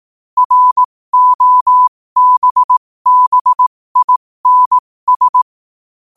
Morse code, heard as a keyed tone, about 18 wpm, spaced as for 15 wpm, 1000 Hz. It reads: ROBBINS
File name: qotdfiles / 15wpm